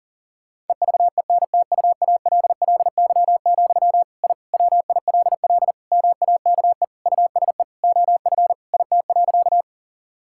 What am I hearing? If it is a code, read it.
Morse code, heard as a keyed tone, about 40 wpm, 700 Hz. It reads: EVENTUALLY, I WILL MAKE USE OF IT.
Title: Sentences from Top 1000 Words - Review Part 6 - 40wpm